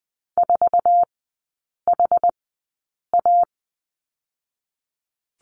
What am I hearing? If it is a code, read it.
Morse code, heard as a keyed tone, about 20 wpm, 700 Hz. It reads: 4 H A
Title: 121 Lesson 20e - Number 3 - Three Characters 20wpm